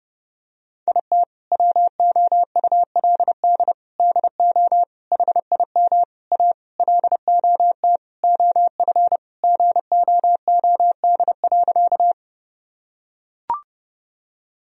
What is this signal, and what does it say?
Morse code, heard as a keyed tone, about 32 wpm, 700 Hz. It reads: IT WOULD DO HIM A LOT OF GOOD.
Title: Sentences from Top 1000 Words - Review Part 3 - 30wpm